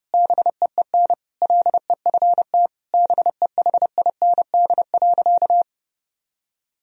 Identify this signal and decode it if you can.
Morse code, heard as a keyed tone, about 30 wpm, 700 Hz. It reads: BEEN LEFT BEHIND.
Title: Sentences from Top 1000 Words - Review Part 1 - 30wpm